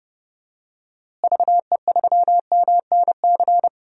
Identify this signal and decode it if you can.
Morse code, heard as a keyed tone, about 30 wpm, 700 Hz. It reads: VE3MNC
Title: DXCC-100 Call Signs - 6 Characters - Repeated 3x - 30wpm